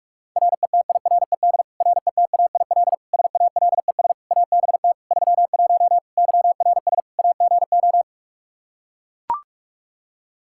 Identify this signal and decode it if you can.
Morse code, heard as a keyed tone, about 41 wpm, 700 Hz. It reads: RETIRED RETAIL SALES ABT 31 YRS AGO
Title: Practice QSO - Exchange 3 - More Info - 45wpm